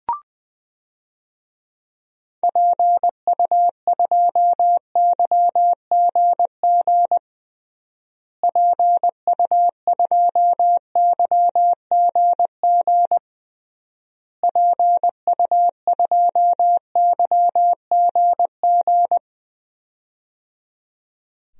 Morse code, heard as a keyed tone, about 20 wpm, 700 Hz. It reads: PU2YGG PU2YGG PU2YGG
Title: DXCC-100 Call Signs - 6 Characters - Repeated 3x - 20wpm